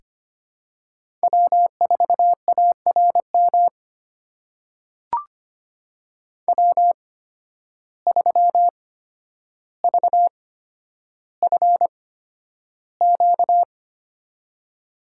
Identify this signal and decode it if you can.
Morse code, heard as a keyed tone, about 25 wpm, spaced as for 11 wpm, 700 Hz. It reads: W4ARM W3VFQ
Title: USA Call Signs - 5 Characters - 8x Character Spacing - 25wpm